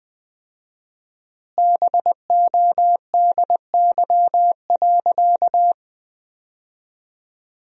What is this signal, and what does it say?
Morse code, heard as a keyed tone, about 20 wpm, 700 Hz. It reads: BODY.
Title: 4x Word-Spacing - Sentences from Top 400 Words - 20wpm